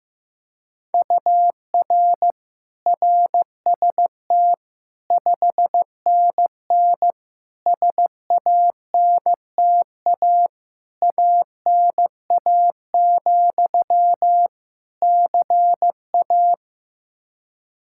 Morse code, heard as a keyed tone, about 16 wpm, 700 Hz. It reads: UR RST 5NN SANTA ANA, CA
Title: Practice QSO - Exchange 1 - Beginning - 15wpm